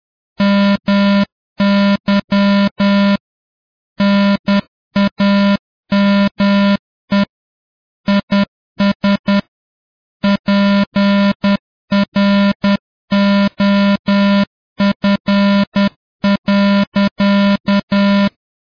Morse code, heard as a keyed tone, about 10 wpm, 200 Hz. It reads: MY NAME IS PROF.